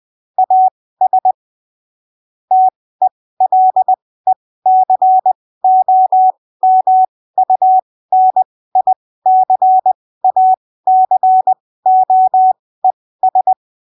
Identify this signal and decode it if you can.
Morse code, heard as a keyed tone, about 20 wpm, spaced as for 15 wpm, 750 Hz. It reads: AS TELECOMUNICACOES
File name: arquivos / texto / texto 25 20ppm